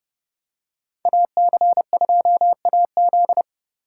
Morse code, heard as a keyed tone, about 30 wpm, 700 Hz. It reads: AC2AZ